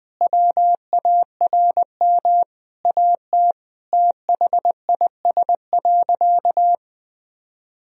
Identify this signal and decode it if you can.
Morse code, heard as a keyed tone, about 20 wpm, 700 Hz. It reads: WARM AT THIS.
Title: Sentences from Top 1000 Words - Review Part 2 - 20wpm